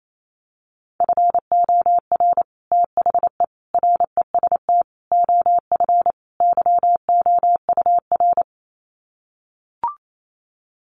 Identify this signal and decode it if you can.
Morse code, heard as a keyed tone, about 28 wpm, 700 Hz. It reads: FOR THE REST OF YOUR
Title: Sets of 5 Words Encore 28wpm